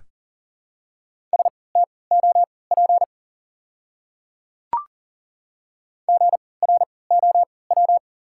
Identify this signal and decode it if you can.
Morse code, heard as a keyed tone, about 40 wpm, spaced as for 23 wpm, 700 Hz. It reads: STOP GROW
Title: 3x Character Spacing - Top 300 Words - 40wpm